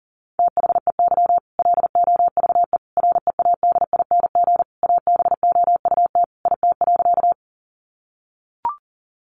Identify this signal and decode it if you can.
Morse code, heard as a keyed tone, about 40 wpm, 700 Hz. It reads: THEY LOVE READING ABOUT IT.
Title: Sentences from Top 600 Words 40wpm